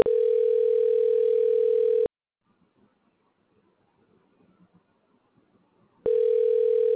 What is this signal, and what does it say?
Ringback tone (North America), 440 + 480 Hz.